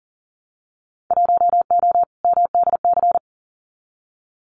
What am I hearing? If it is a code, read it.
Morse code, heard as a keyed tone, about 42 wpm, 700 Hz. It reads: 1O MDC
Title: Winter Field Day Exchanges 40wpm